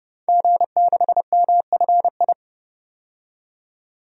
Morse code, heard as a keyed tone, about 30 wpm, 700 Hz. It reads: G6MFI
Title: DXCC-100 Call Signs - 5 Characters - 30wpm